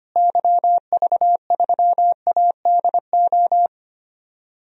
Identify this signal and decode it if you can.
Morse code, heard as a keyed tone, about 25 wpm, 700 Hz. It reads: YV3ADO
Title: DXCC-100 Call Signs - 6 Characters - Repeated 3x - 25wpm